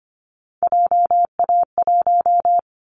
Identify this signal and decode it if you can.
Morse code, heard as a keyed tone, about 25 wpm, 700 Hz. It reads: JA1
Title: Callsign Prefixes for Top 25 DXCC - 25wpm